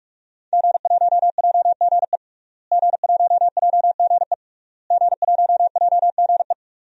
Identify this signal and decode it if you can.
Morse code, heard as a keyed tone, about 42 wpm, 700 Hz. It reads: G1JGE G1JGE G1JGE
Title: DXCC-100 Call Signs - 5 Characters - Repeated 3x - 45wpm